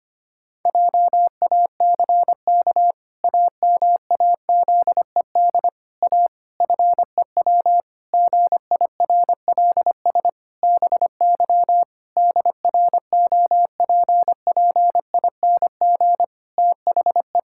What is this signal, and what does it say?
Morse code, heard as a keyed tone, about 25 wpm, 700 Hz. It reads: JACK AMAZED A FEW GIRLS BY DROPPING THE
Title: Pangram Challenge v2 - 25wpm